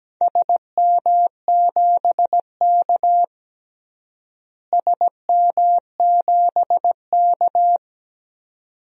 Morse code, heard as a keyed tone, about 17 wpm, 700 Hz. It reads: SM7K SM7K
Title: DXCC-100 Call Signs - 4 Characters - Repeated 2x - 17wpm